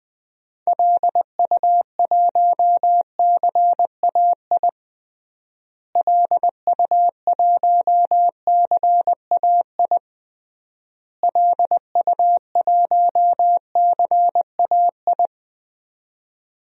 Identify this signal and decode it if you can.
Morse code, heard as a keyed tone, about 20 wpm, 700 Hz. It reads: LU1CAI LU1CAI LU1CAI
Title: DXCC-100 Call Signs - 6 Characters - Repeated 3x - 20wpm